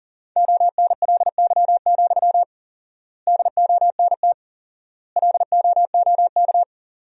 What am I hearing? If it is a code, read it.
Morse code, heard as a keyed tone, about 40 wpm, 700 Hz. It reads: ONLY, DONT LOOK
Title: 4x Word-Spacing - Sentences from Top 300 Words - 40wpm